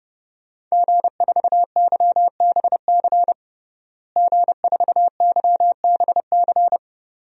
Morse code, heard as a keyed tone, about 30 wpm, 700 Hz. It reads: G4YBC G4YBC
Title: DXCC-100 Call Signs - 5 Characters - Repeated 2x - 30wpm